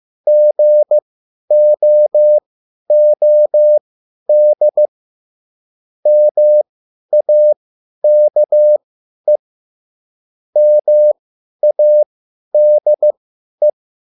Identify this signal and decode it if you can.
Morse code, heard as a keyed tone, about 15 wpm, spaced as for 10 wpm, 600 Hz. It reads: GOOD MAKE MADE
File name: cwbook